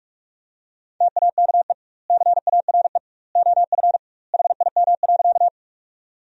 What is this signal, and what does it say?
Morse code, heard as a keyed tone, about 46 wpm, 700 Hz. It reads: TAKE CARE OF HIM.
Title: 2x Word-Spacing - Sentences from Top 300 Words - 45wpm